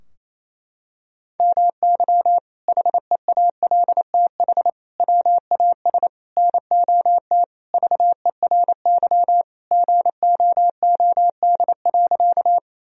Morse code, heard as a keyed tone, about 27 wpm, 700 Hz. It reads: MY HEALTH WAS NOT VERY GOOD.
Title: Sentences from Top 1000 Words - Review Part 3 - 28wpm